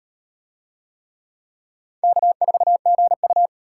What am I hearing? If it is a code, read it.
Morse code, heard as a keyed tone, about 34 wpm, 700 Hz. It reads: K4GU